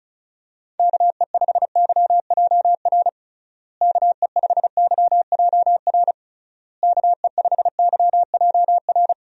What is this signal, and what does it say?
Morse code, heard as a keyed tone, about 34 wpm, 700 Hz. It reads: KE5YJR KE5YJR KE5YJR
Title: USA Call Signs - 6 Characters - Repeated 3x - 35wpm